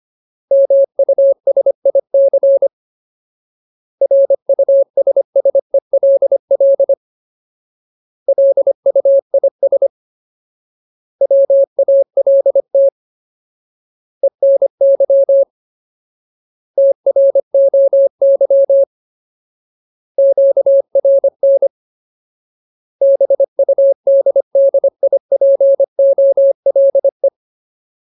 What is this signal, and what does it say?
Morse code, heard as a keyed tone, about 25 wpm, 550 Hz. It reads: MUSIC RUSSELL LUIS WALT ENY TROY QRN BUDDIPOLE
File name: Ultimate Word List / 25WPM Ultimate